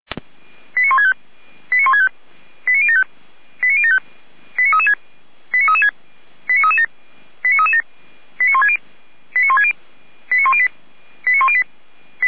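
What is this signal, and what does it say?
Five-tone selective call (ZVEI), heard as quick runs of five tones, tones of about 80 ms. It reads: 89165 89165 89075 89075 89336 89336 89337 89337 89160 89160 89108 89108
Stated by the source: ZVEI five-tone alert received by SDR, from feuerwehr-orlamuende.org fire brigade (Orlamünde).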